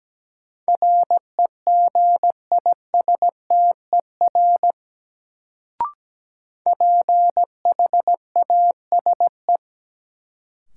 Morse code, heard as a keyed tone, about 17 wpm, 700 Hz. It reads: REGISTER PHASE
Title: Sentences from Top 1200 Words - New Word Drill Alt - 17wpm